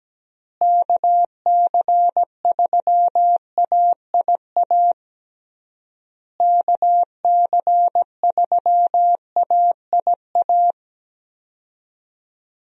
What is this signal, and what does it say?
Morse code, heard as a keyed tone, about 17 wpm, 700 Hz. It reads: KC3AIA KC3AIA
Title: USA Call Signs - 6 Characters - Repeated 3x - 17wpm